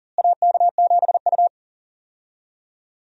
Morse code, heard as a keyed tone, about 40 wpm, 700 Hz. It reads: AK7U
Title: USA Call Signs - 4 Characters - Repeated 2x - 40wpm